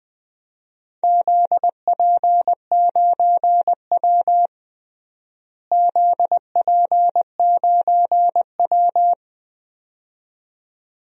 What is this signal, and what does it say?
Morse code, heard as a keyed tone, about 20 wpm, 700 Hz. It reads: ZP9W ZP9W
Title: DXCC-100 Call Signs - 4 Characters - Repeated 2x - 20wpm